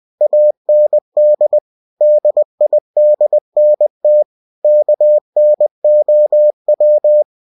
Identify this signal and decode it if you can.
Morse code, heard as a keyed tone, about 20 wpm, 600 Hz. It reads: AND DIDNT KNOW